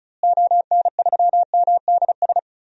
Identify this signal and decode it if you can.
Morse code, heard as a keyed tone, about 34 wpm, 700 Hz. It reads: ON3MDS
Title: DXCC-100 Call Signs - 6 Characters - Repeated 2x - 35wpm